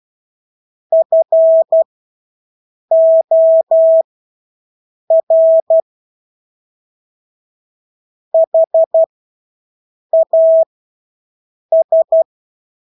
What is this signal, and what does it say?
Morse code, heard as a keyed tone, about 12 wpm, spaced as for 6 wpm, 650 Hz. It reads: FOR HAS
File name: Engelske ord 12-6 wpm